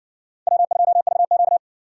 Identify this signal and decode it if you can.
Morse code, heard as a keyed tone, about 57 wpm, 700 Hz. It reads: W2VQ